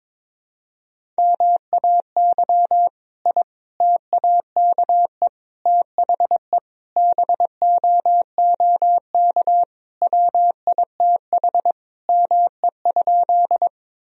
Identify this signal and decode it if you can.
Morse code, heard as a keyed tone, about 22 wpm, 700 Hz. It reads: MAY I TAKE THE BOOK WITH ME?
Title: Sentences from Top 1000 Words - Review Part 1 - 22wpm